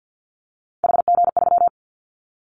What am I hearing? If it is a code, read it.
Morse code, heard as a keyed tone, about 50 wpm, 700 Hz. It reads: HG3
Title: Callsign Prefixes for Top 25 DXCC - 50wpm